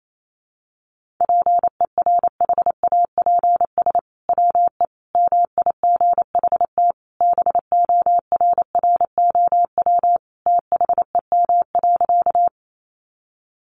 Morse code, heard as a keyed tone, about 28 wpm, 700 Hz. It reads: PERHAPS WE MIGHT BORROW THEM.
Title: Sentences from Top 1500 Words 28wpm